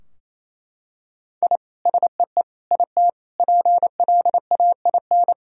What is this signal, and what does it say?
Morse code, heard as a keyed tone, about 28 wpm, 700 Hz. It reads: I SEE IT PLAIN
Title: Sentences from Top 1000 Words - Review Part 2 - 28wpm